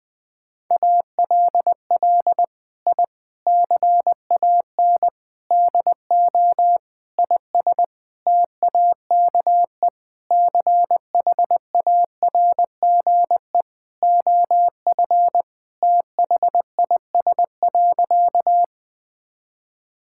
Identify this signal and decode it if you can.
Morse code, heard as a keyed tone, about 20 wpm, 700 Hz. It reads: ALL I CAN DO IS TAKE CHARGE OF THIS.
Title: Sentences from Top 1000 Words - Review Part 6 - 20wpm